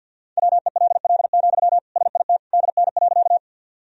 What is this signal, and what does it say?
Morse code, heard as a keyed tone, about 49 wpm, 700 Hz. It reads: WELL, SIT DN.
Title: Sentences from Top 500 Words and Abbreviations 50wpm